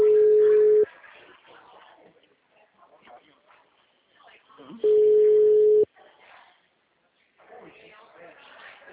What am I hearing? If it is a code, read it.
Ringback tone (Europe, 425 Hz), with speech.